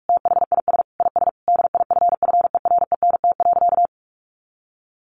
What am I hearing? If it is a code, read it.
Morse code, heard as a keyed tone, about 44 wpm, 700 Hz. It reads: THIS IS DIFFERENT.
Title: Sentences from Top 600 Words 45wpm